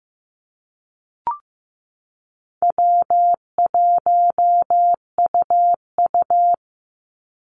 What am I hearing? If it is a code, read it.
Morse code, heard as a keyed tone, about 15 wpm, 700 Hz. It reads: W1UU
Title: Easy Call Signs Repeated 2x - 15wpm